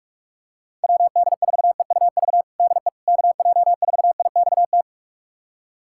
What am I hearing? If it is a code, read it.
Morse code, heard as a keyed tone, about 46 wpm, 700 Hz. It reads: WD4EUV DE KJ4IXT
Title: USA Callsign DE Callsign - 6 Characters - 45wpm